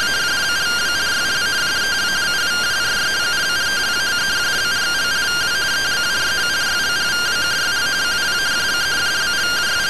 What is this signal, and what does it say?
Two-tone teleprinter signal, about 170 Hz shift.
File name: findings / MR AXIAL 14.gen-19 rtty rx